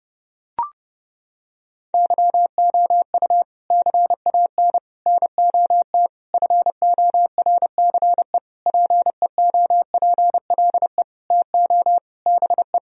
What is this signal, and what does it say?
Morse code, heard as a keyed tone, about 30 wpm, 700 Hz. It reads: YOU CAN NOT FORCE PEOPLE TO BE
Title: Sentences from Top 1000 Words - Review Part 3 - 30wpm